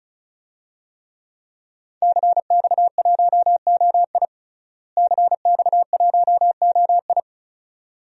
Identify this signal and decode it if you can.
Morse code, heard as a keyed tone, about 36 wpm, 700 Hz. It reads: CX1OI CX1OI
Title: DXCC-100 Call Signs - 5 Characters - Repeated 2x - 35wpm